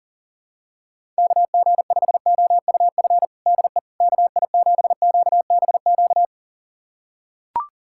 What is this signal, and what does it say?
Morse code, heard as a keyed tone, about 40 wpm, 700 Hz. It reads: KG5OUF DE KI7QBQ
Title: USA Callsign DE Callsign 40wpm